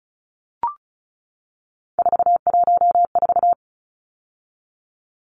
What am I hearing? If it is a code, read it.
Morse code, heard as a keyed tone, about 34 wpm, 700 Hz. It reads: 414
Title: Numbers - 3-Digit 35wpm